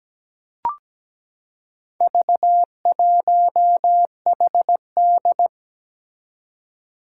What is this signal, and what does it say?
Morse code, heard as a keyed tone, about 17 wpm, 700 Hz. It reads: V1HD
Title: Four Challenging Letter-Number Combinations 17wpm